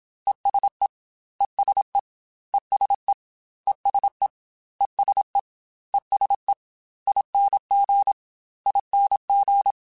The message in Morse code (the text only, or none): ESE ESE ESE ESE ESE ESE ING ING